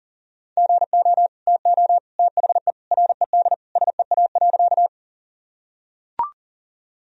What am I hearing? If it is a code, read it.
Morse code, heard as a keyed tone, about 40 wpm, 700 Hz. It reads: GO TO THE RED SEA.
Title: Sentences from Top 1000 Words - Review Part 2 - 40wpm